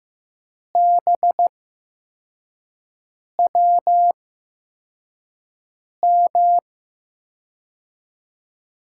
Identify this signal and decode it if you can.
Morse code, heard as a keyed tone, about 15 wpm, spaced as for 4 wpm, 700 Hz. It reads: BWM
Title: USA Call Signs - 6 Characters - 8x Character Spacing - 15wpm